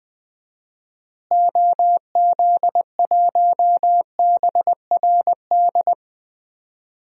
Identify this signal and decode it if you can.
Morse code, heard as a keyed tone, about 21 wpm, 700 Hz. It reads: OZ1BRD
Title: DXCC-100 Call Signs - 6 Characters - Repeated 3x - 20wpm